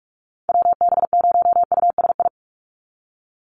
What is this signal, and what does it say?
Morse code, heard as a keyed tone, about 46 wpm, 700 Hz. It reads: WB0USI
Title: NA SOTA Activator Callsigns 45wpm